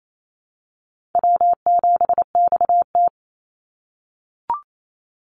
Morse code, heard as a keyed tone, about 28 wpm, 700 Hz. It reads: W7XT
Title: Easy Call Signs - 28wpm